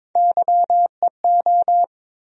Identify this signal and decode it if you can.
Morse code, heard as a keyed tone, about 22 wpm, 700 Hz. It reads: YEO
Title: USA Call Sign Postfixes - 3 Letters 22wpm